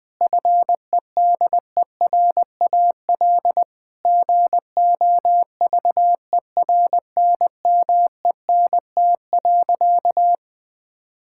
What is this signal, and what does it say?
Morse code, heard as a keyed tone, about 20 wpm, 700 Hz. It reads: FEDERAL GOVERNMENT.
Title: Sentences from Top 1000 Words - Review Part 5 - 20wpm